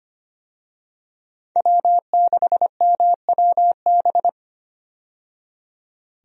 Morse code, heard as a keyed tone, about 25 wpm, 700 Hz. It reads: W6MWB
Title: USA Call Signs - 5 Characters - 25wpm